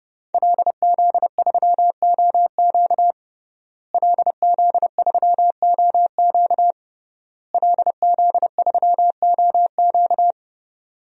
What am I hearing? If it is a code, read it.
Morse code, heard as a keyed tone, about 30 wpm, 700 Hz. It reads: LZ3OQ LZ3OQ LZ3OQ
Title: DXCC-100 Call Signs - 5 Characters - Repeated 3x - 30wpm